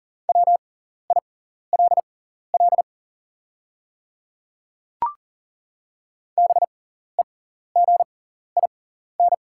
Morse code, heard as a keyed tone, about 40 wpm, spaced as for 14 wpm, 700 Hz. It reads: WILL BEGIN
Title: 6x Character Spacing - Top 300 Words - 40wpm